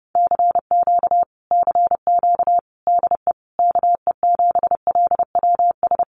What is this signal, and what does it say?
Morse code, heard as a keyed tone, about 30 wpm, 700 Hz. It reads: CQ CQ DE KE7LWS
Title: Common QSO Phrases Encore 30wpm